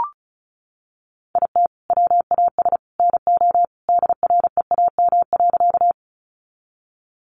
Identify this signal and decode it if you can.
Morse code, heard as a keyed tone, about 34 wpm, 700 Hz. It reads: IT WAS NO DREAM.
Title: Sentences from Top 1500 Words 35wpm